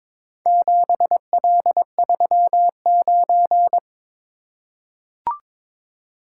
Morse code, heard as a keyed tone, about 22 wpm, 700 Hz. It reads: 7L39
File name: practice / 22wpm / Four Letter-Number Combinations - 22wpm